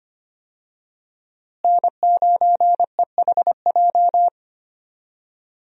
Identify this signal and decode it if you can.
Morse code, heard as a keyed tone, about 25 wpm, 700 Hz. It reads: N9EHJ
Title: USA Call Signs - 5 Characters - Repeated 2x - 25wpm